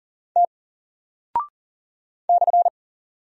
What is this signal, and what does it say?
Morse code, heard as a keyed tone, about 40 wpm, 700 Hz. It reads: T /
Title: Single Character (Letter:Number:Punctuation:Prosign) Alt 40wpm